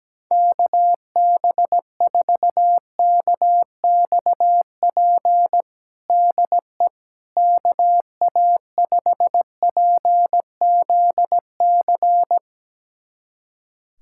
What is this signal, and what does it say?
Morse code, heard as a keyed tone, about 17 wpm, 700 Hz. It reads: KB4KXP DE KA5PZC
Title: USA Callsign DE Callsign - 6 Characters - 17wpm